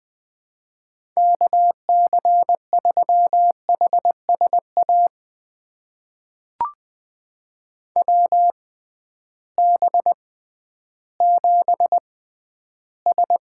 Morse code, heard as a keyed tone, about 20 wpm, spaced as for 12 wpm, 700 Hz. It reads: KC3HSA WB7S
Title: USA Call Signs - 6 Characters - 6x Character Spacing - 20wpm